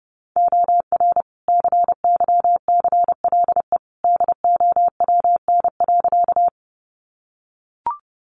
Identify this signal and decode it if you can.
Morse code, heard as a keyed tone, about 30 wpm, 700 Hz. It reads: OR CYCLE DOWN.
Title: Sentences from Top 900 Words 30wpm